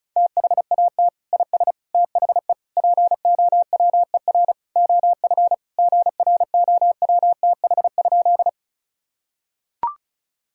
Morse code, heard as a keyed tone, about 36 wpm, 700 Hz. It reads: THAT IS THE POWER OF GROWTH?
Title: Sentences from Top 1000 Words - Review Part 5 - 35wpm